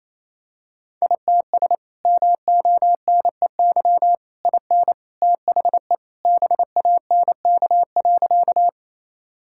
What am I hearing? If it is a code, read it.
Morse code, heard as a keyed tone, about 27 wpm, 700 Hz. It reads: ITS MONEY IN THE BANK.
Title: Sentences from Top 1000 Words - Review Part 6 - 28wpm